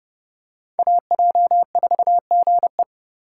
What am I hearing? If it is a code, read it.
Morse code, heard as a keyed tone, about 30 wpm, 700 Hz. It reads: AJ4GE
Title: USA Call Signs - 2x Character Spacing - 30wpm